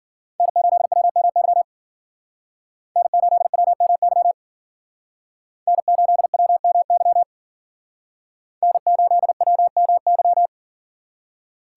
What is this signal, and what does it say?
Morse code, heard as a keyed tone, about 53 wpm, 700 Hz. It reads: N8WMY N8WMY N8WMY N8WMY